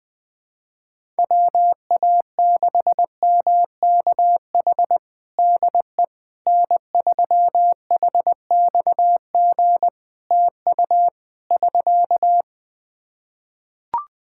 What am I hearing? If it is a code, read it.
Morse code, heard as a keyed tone, about 21 wpm, 700 Hz. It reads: WA6MKH DE N3HXG TU <SK>
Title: Practice QSO - Exchange 4 - Ending - 20wpm